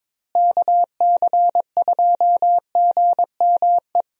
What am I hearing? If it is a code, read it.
Morse code, heard as a keyed tone, about 22 wpm, 700 Hz. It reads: KC2GME
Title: USA Call Signs - 6 Characters - Repeated 2x - 22wpm